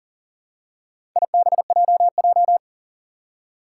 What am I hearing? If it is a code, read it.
Morse code, heard as a keyed tone, about 40 wpm, 700 Hz. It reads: IBJJ
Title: Four Challenging Letter-Number Combinations 40wpm